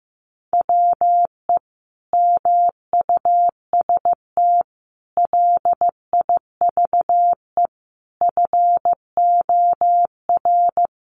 Morse code, heard as a keyed tone, about 15 wpm, 700 Hz. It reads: WE MUST LIVE FOR
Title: Sentences from Top 700 Words 15wpm